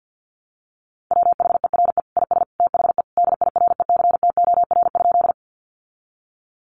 Morse code, heard as a keyed tone, about 50 wpm, 700 Hz. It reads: WHERE IS THE DIRECTOR?
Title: Sentences from Top 1300 Words 50wpm